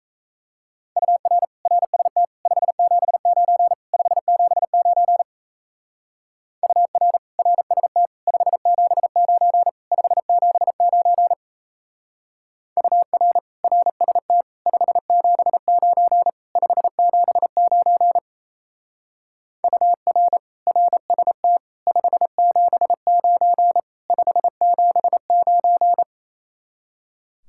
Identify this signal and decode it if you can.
Morse code, heard as a keyed tone, about 33 wpm, 700 Hz. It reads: UR RST 579 579 UR RST 579 579 UR RST 579 579 UR RST 579 579
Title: Speed-Racing Common QSO Phrases - 28wpm